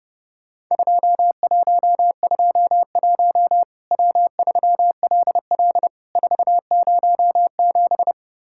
Morse code, heard as a keyed tone, about 30 wpm, 700 Hz. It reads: 2121 W3LL 407
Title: Worked All Europe DX Contest - QTC Exchange - 30wpm